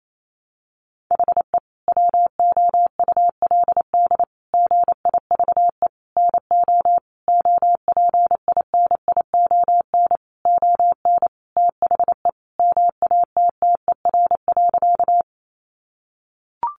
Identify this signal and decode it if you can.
Morse code, heard as a keyed tone, about 28 wpm, 700 Hz. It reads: HE WOULD GIVE NO OPINION ON THE MATTER.
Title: Sentences from Top 1200 Words 28wpm